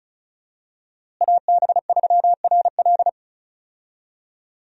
Morse code, heard as a keyed tone, about 36 wpm, 700 Hz. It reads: AB3RL
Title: USA Call Signs - 5 Characters - Repeated 3x - 35wpm